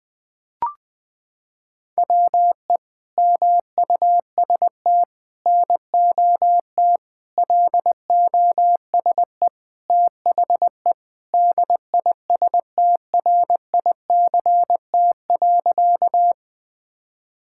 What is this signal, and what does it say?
Morse code, heard as a keyed tone, about 20 wpm, 700 Hz. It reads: WE MUST NOT LOSE THE DISTRICT.
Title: Sentences from Top 1600 Words 20wpm